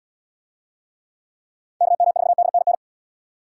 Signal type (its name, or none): Morse code